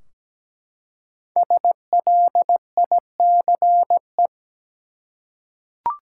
Morse code, heard as a keyed tone, about 17 wpm, 700 Hz. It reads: SLICE
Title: Five Letter Words - 17wpm